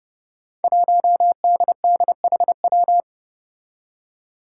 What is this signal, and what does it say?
Morse code, heard as a keyed tone, about 30 wpm, 700 Hz. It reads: 1DDHW
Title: Five Challenging Letter-Number Combinations 30wpm